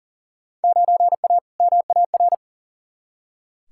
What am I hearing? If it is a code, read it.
Morse code, heard as a keyed tone, about 39 wpm, 700 Hz. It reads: 9A MAR
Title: Field Day Exchanges Encore - 40wpm